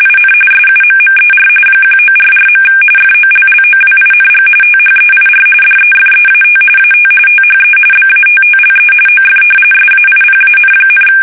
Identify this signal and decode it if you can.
RTTY, about 45.45 baud. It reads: IN SUMMER 1944. FOLLOWING THEIR DEFEAT AT THE BATTLE OF SAIPAN,
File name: fsk850